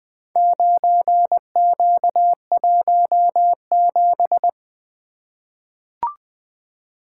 Morse code, heard as a keyed tone, about 20 wpm, 700 Hz. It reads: 9Q17
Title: Four Letter-Number Combinations - 20wpm